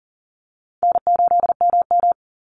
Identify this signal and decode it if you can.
Morse code, heard as a keyed tone, about 40 wpm, 700 Hz. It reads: N8MM
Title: USA Call Signs 40wpm